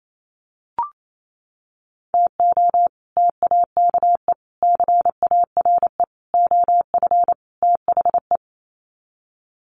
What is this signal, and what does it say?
Morse code, heard as a keyed tone, about 28 wpm, 700 Hz. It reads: TO TAKE CARE OF THE
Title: Sets of 5 Words Encore 28wpm